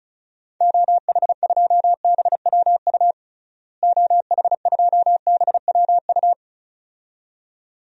Morse code, heard as a keyed tone, about 36 wpm, 700 Hz. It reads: OH2BWU OH2BWU
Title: DXCC-100 Call Signs - 6 Characters - Repeated 2x - 35wpm